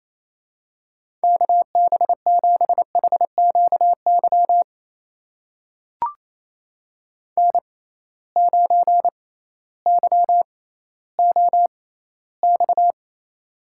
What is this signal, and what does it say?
Morse code, heard as a keyed tone, about 28 wpm, spaced as for 15 wpm, 700 Hz. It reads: KB7HQY N9YOX
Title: USA Call Signs - 6x Character Spacing - 28wpm